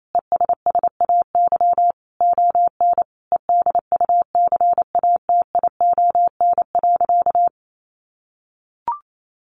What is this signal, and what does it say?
Morse code, heard as a keyed tone, about 28 wpm, 700 Hz. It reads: ESSAY ON EDUCATION.
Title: Sentences from Top 1500 Words 28wpm